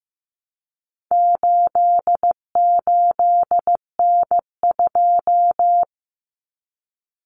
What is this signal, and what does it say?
Morse code, heard as a keyed tone, about 15 wpm, 700 Hz. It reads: 88N2
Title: Common Cut Numbers - 4-Digit 15wpm